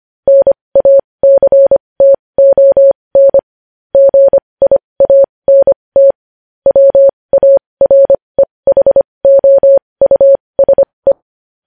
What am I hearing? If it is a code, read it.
Morse code, heard as a keyed tone, about 25 wpm, spaced as for 20 wpm, 550 Hz. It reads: NACTON GIANT WAREHOUSE
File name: podcast / News headlines-1x-25x20